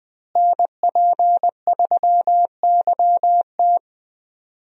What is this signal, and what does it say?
Morse code, heard as a keyed tone, about 20 wpm, 700 Hz. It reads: NP3YT
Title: USA Call Signs - 5 Characters - Repeated 2x - 20wpm